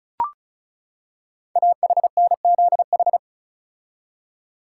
Morse code, heard as a keyed tone, about 36 wpm, 690 Hz. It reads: AHNZH
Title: Code Groups - 5 Letters - 35wpm